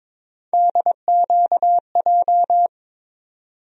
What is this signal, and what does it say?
Morse code, heard as a keyed tone, about 22 wpm, 700 Hz. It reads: DQJ
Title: USA Call Sign Postfixes - 3 Letters 22wpm